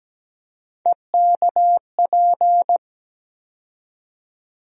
Morse code, heard as a keyed tone, about 17 wpm, 700 Hz. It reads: EKP
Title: USA Call Sign Postfixes - 3 Letters 17wpm